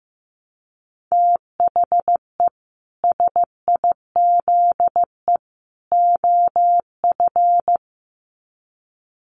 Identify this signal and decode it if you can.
Morse code, heard as a keyed tone, about 15 wpm, 700 Hz. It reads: THE SIZE OF
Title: Sets of 3 Words 15wpm